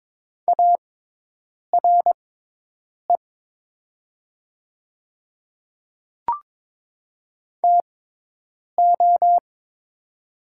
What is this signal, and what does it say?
Morse code, heard as a keyed tone, about 22 wpm, spaced as for 8 wpm, 700 Hz. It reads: ARE TO
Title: 6x Character Spacing - Top 100 Words - 22wpm